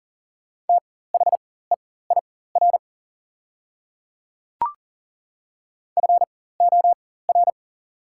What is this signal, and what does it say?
Morse code, heard as a keyed tone, about 38 wpm, spaced as for 18 wpm, 690 Hz. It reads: THEIR FOR
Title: 4x Character Spacing - Top 200 Words - 40wpm